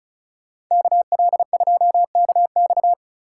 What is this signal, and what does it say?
Morse code, heard as a keyed tone, about 36 wpm, 700 Hz. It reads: KL2KX
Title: Speed Racing - USA 2x2 Call Signs 30wpm